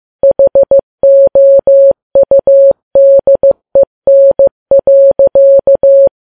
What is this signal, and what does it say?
Morse code, heard as a keyed tone, about 15 wpm, 550 Hz. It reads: HOUDEN.
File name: Files Web tls / Morse on9fv / morse15kps